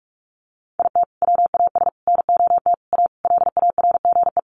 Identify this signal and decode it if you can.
Morse code, heard as a keyed tone, about 44 wpm, 700 Hz. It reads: IT WAS NOT A LARGE
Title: Sentences from Top 1400 Words 45wpm